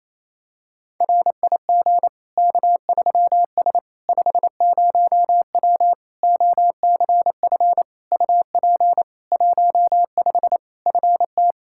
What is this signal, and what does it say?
Morse code, heard as a keyed tone, about 28 wpm, 700 Hz. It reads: RIG K3S 50W OCF UP 15 FT
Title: Practice QSO - Exchange 2 - Info - 28wpm